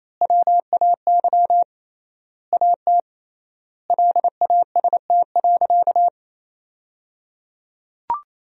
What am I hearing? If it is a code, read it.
Morse code, heard as a keyed tone, about 28 wpm, 700 Hz. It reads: WAY AT LAST.
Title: 3x Word-Spacing - Sentences from Top 300 Words - 28wpm